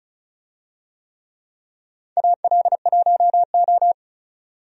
Morse code, heard as a keyed tone, about 34 wpm, 700 Hz. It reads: AL1O
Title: USA Call Signs - 4 Characters - 35wpm